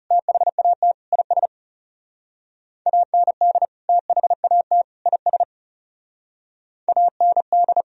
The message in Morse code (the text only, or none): THAT IS AND THAT IS AND